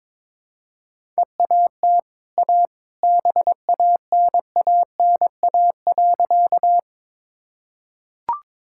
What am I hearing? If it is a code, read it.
Morse code, heard as a keyed tone, about 22 wpm, 690 Hz. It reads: EAT A BANANA.
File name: practice / 22wpm / Easy Sentences with Hints - 22wpm